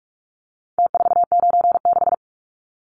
Morse code, heard as a keyed tone, about 44 wpm, 700 Hz. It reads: T496
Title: Numbers - Contest Style - 4-Digit 45wpm